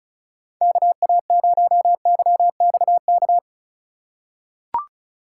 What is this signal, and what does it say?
Morse code, heard as a keyed tone, about 34 wpm, 700 Hz. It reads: KA0YXK
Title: USA Call Signs - 6 Characters - 3x Character Spacing - 35wpm